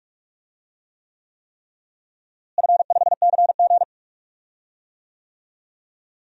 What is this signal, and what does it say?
Morse code, heard as a keyed tone, about 46 wpm, 700 Hz. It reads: F5CG
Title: DXCC-100 Call Signs - 4 Characters - Repeated 3x - 45wpm